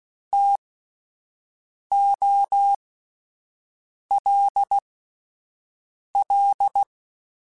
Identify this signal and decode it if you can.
Morse code, heard as a keyed tone, about 16 wpm, spaced as for 5 wpm, 800 Hz. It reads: TOLL
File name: Alphabetical / Alphabetical mp3s / Ts @5